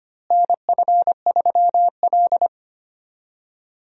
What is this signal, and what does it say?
Morse code, heard as a keyed tone, about 25 wpm, 700 Hz. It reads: NF3L
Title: USA Call Signs - 4 Characters - Repeated 2x - 25wpm